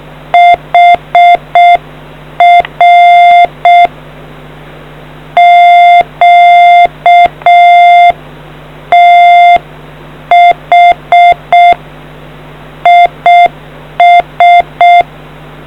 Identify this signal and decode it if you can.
Morse code, heard as a keyed tone, about 6 wpm, 700 Hz. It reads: HR QTH IS